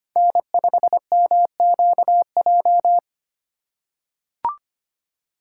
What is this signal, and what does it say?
Morse code, heard as a keyed tone, about 25 wpm, 700 Hz. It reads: N5MQJ
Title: USA Call Signs - 5 Characters - 8x Character Spacing - 25wpm